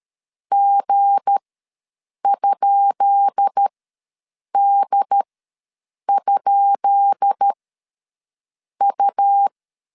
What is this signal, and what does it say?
Morse code, heard as a keyed tone, about 13 wpm, spaced as for 7 wpm, 800 Hz. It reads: G?D?U
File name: LEZIONI / 10 SIGNS / 10B INT ws 12wpm 3527